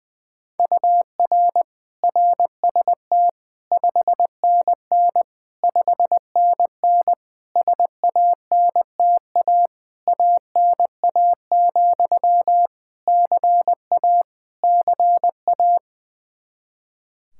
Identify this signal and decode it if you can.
Morse code, heard as a keyed tone, about 20 wpm, 700 Hz. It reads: UR RST 5NN 5NN SANTA ANA, CA CA
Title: Practice QSO - Exchange 1 - Beginning - 20wpm